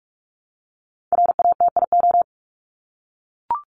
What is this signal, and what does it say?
Morse code, heard as a keyed tone, about 44 wpm, 700 Hz. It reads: RATIO
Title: Top 1751-2000 Words 45wpm